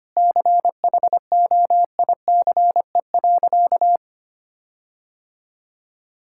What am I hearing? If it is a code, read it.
Morse code, heard as a keyed tone, about 25 wpm, 700 Hz. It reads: CHOICE.
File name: practice / 25wpm / Sentences with 4-Word Phrases - 25wpm